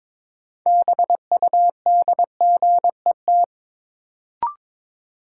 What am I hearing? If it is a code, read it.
Morse code, heard as a keyed tone, about 22 wpm, 700 Hz. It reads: BUDGET
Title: Sentences from Top 1200 Words - New Word Drill Alt - 22wpm